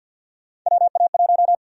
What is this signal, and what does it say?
Morse code, heard as a keyed tone, about 50 wpm, 700 Hz. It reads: WA1
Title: USA Call Sign Prefixes - Repeated 2x - 50wpm